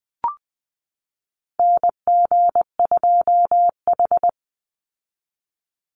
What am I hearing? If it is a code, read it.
Morse code, heard as a keyed tone, about 20 wpm, 700 Hz. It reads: NG2H
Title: Call Signs Repeated 3x - 20wpm